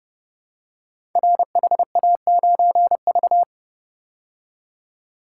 Morse code, heard as a keyed tone, about 30 wpm, 700 Hz. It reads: RHA9V
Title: Code Groups - 5 Letter:Numbers - 30wpm